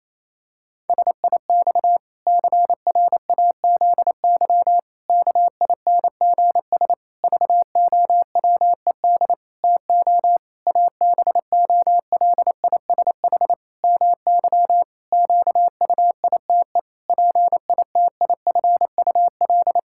Morse code, heard as a keyed tone, about 28 wpm, 700 Hz. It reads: SIX CRAZY KINGS VOWED TO ABOLISH MY QUITE PITIFUL
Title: Pangram Challenge v2 - 28wpm